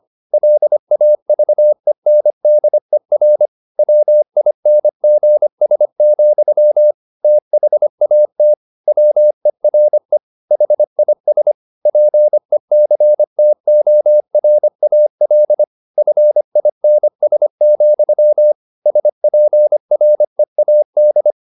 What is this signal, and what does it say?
Morse code, heard as a keyed tone, about 25 wpm, 600 Hz. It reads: LAVENDER WINGS, THAT WERE HIS PECTORAL FINS, SPREAD
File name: the old man and the sea part 1 25wpm